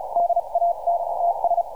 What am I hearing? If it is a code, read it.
Morse code, heard as a keyed tone, about 34 wpm, spaced as for 30 wpm, 690 Hz. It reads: DAT TS
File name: archived / CRSnet